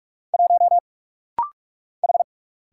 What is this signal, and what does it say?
Morse code, heard as a keyed tone, about 45 wpm, 700 Hz. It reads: 1 H
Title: Single Letter-Number Alt - Rapid-Fire 45wpm